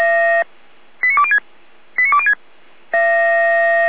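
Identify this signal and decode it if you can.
Five-tone selective call (ZVEI), heard as quick runs of five tones, tones of about 80 ms. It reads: 89206 89206
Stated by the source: ZVEI five-tone selective call received by SDR.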